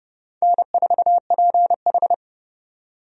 Morse code, heard as a keyed tone, about 30 wpm, 700 Hz. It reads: N4PH
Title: USA Call Signs - 4 Characters - 30wpm